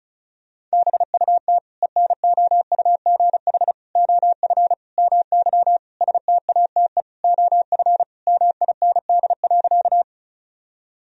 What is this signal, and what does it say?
Morse code, heard as a keyed tone, about 36 wpm, 700 Hz. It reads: BUT ENOUGH OF MY STATE OF MIND.
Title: Sentences from Top 1000 Words - Review Part 2 - 35wpm